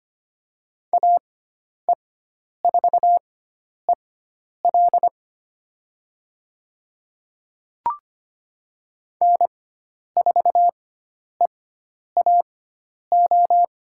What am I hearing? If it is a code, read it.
Morse code, heard as a keyed tone, about 25 wpm, spaced as for 10 wpm, 700 Hz. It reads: AE4EL N4EAO
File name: Lesson 16g - Letter c - Call Signs 25wpm